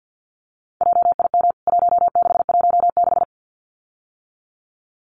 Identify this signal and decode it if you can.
Morse code, heard as a keyed tone, about 50 wpm, 700 Hz. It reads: JIM 1616
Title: CWT Exchange 50wpm